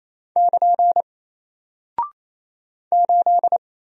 Morse code, heard as a keyed tone, about 27 wpm, 700 Hz. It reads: <KN> 8
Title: Single Character (Letter:Number:Punctuation:Prosign) 28wpm